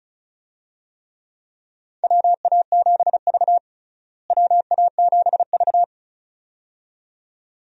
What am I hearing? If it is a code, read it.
Morse code, heard as a keyed tone, about 36 wpm, 700 Hz. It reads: WA7V WA7V
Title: USA Call Signs - 4 Characters - Repeated 2x - 35wpm